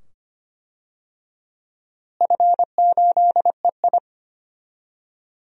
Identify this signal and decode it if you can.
Morse code, heard as a keyed tone, about 25 wpm, 700 Hz. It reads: F8EI
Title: DXCC-100 Call Signs - 4 Characters - Repeated 2x - 25wpm